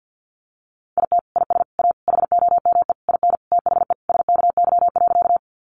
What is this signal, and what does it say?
Morse code, heard as a keyed tone, about 50 wpm, 700 Hz. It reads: IT IS A HOME IN THE SKY.
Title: Sentences from Top 1200 Words 50wpm